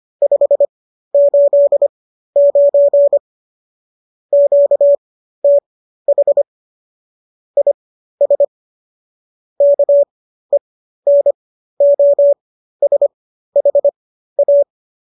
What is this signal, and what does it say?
Morse code, heard as a keyed tone, about 25 wpm, spaced as for 13 wpm, 570 Hz. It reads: 589 QTH IS KENOSHA